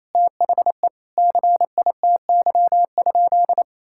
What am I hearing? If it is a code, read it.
Morse code, heard as a keyed tone, about 28 wpm, 700 Hz. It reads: THE CITY?
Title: Sentences from Top 1000 Words - Review Part 5 - 28wpm